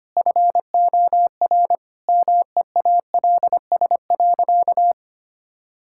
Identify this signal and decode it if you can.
Morse code, heard as a keyed tone, about 25 wpm, 700 Hz. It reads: FOR MEALS.
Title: Pangram Challenge v2 - 25wpm